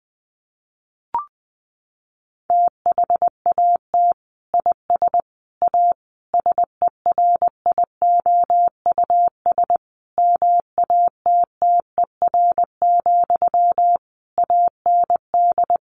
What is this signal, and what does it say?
Morse code, heard as a keyed tone, about 20 wpm, 700 Hz. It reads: THAT IS A SERIOUS MATTER, AND
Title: Sentences from Top 1100 Words 20wpm